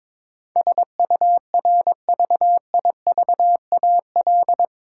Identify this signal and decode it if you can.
Morse code, heard as a keyed tone, about 22 wpm, 700 Hz. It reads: SURVIVAL
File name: practice / 22wpm / Eight Letter Words - 22wpm